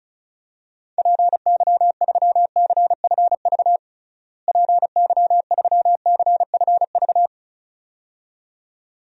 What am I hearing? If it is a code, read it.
Morse code, heard as a keyed tone, about 34 wpm, 700 Hz. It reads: PY3CFV PY3CFV
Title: DXCC-100 Call Signs - 6 Characters - Repeated 2x - 35wpm